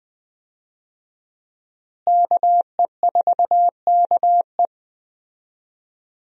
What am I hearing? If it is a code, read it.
Morse code, heard as a keyed tone, about 20 wpm, 700 Hz. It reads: KE4KE
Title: USA Call Signs - 5 Characters - Repeated 2x - 20wpm